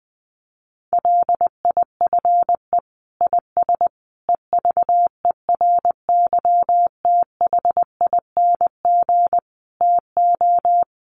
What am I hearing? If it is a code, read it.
Morse code, heard as a keyed tone, about 20 wpm, 700 Hz. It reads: LIFE IS EVERYTHING TO
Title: Sentences from Top 700 Words 20wpm